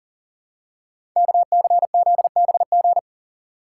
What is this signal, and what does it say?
Morse code, heard as a keyed tone, about 42 wpm, 700 Hz. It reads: KCZBG
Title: Code Groups - 5 Letters - 40wpm